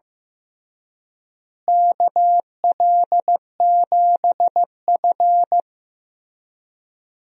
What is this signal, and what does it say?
Morse code, heard as a keyed tone, about 16 wpm, 700 Hz. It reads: KL7F
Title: DXCC-100 Call Signs - 4 Characters - Repeated 3x - 15wpm